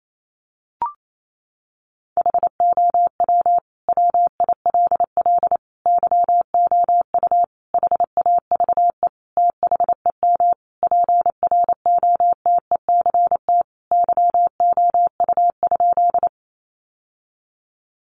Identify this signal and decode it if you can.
Morse code, heard as a keyed tone, about 28 wpm, 690 Hz. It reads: HOW WILL YOU HAVE THEM PROTECT YOU?
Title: Sentences from Top 900 Words 28wpm